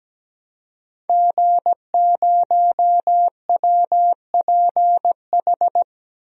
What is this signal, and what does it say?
Morse code, heard as a keyed tone, about 17 wpm, 700 Hz. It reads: G0WPH
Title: DXCC-100 Call Signs - 5 Characters - Repeated 3x - 17wpm